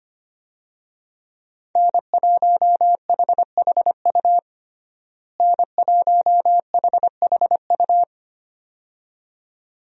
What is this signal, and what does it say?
Morse code, heard as a keyed tone, about 25 wpm, 700 Hz. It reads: N1HHU N1HHU
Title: USA Call Signs - 5 Characters - Repeated 2x - 25wpm